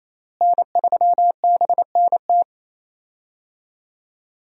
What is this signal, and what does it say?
Morse code, heard as a keyed tone, about 28 wpm, 700 Hz. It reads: N3BNT
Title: USA Call Signs - 5 Characters - Repeated 2x - 28wpm